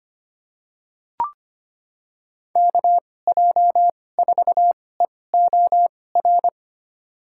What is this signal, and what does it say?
Morse code, heard as a keyed tone, about 25 wpm, spaced as for 18 wpm, 700 Hz. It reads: KJ4EOR
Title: USA Call Signs - 6 Characters - 2x Character Spacing - 25wpm